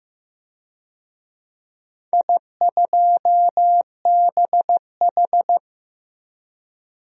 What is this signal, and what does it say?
Morse code, heard as a keyed tone, about 15 wpm, 700 Hz. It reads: I2BH
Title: DXCC-100 Call Signs - 4 Characters - 15wpm